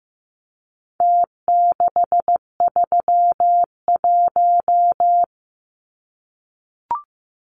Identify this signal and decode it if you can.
Morse code, heard as a keyed tone, about 15 wpm, 700 Hz. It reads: T631
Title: Numbers - Contest Style - 4-Digit 15wpm